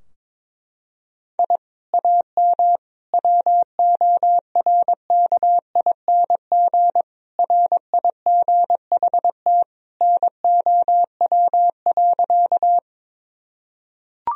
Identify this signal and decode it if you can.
Morse code, heard as a keyed tone, about 22 wpm, 700 Hz. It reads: I AM WORKING RIGHT NOW.
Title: Sentences from Top 1000 Words - Review Part 4 - 22wpm